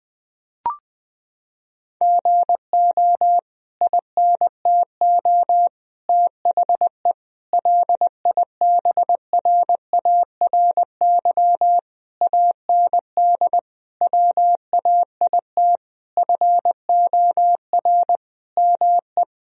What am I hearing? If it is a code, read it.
Morse code, heard as a keyed tone, about 20 wpm, 700 Hz. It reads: GO INTO THE LIBRARY AND WAIT FOR ME
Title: Sentences from Top 1000 Words - Review Part 4 - 20wpm